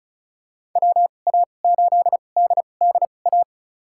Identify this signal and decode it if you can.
Morse code, heard as a keyed tone, about 34 wpm, spaced as for 25 wpm, 700 Hz. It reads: WA8DDA
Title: USA Call Signs - 2x Character Spacing - 35wpm